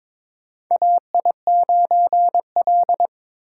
Morse code, heard as a keyed tone, about 22 wpm, 700 Hz. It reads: AI9L
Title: USA Call Signs - 4 Characters - 8x Character Spacing - 22wpm